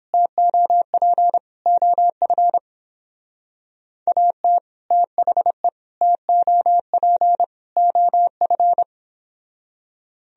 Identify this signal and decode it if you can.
Morse code, heard as a keyed tone, about 29 wpm, 700 Hz. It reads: TOP OF AT THE TOP OF